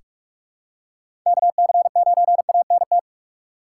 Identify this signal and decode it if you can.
Morse code, heard as a keyed tone, about 42 wpm, 700 Hz. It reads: KC9ANT
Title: USA Call Signs - 6 Characters - 4x Character Spacing - 45wpm